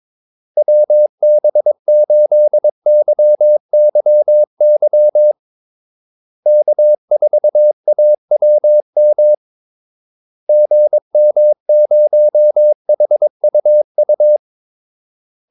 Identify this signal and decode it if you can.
Morse code, heard as a keyed tone, about 23 wpm, 600 Hz. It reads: WB8YYY K4AWM GM0HUU
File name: Callsigns-22wpm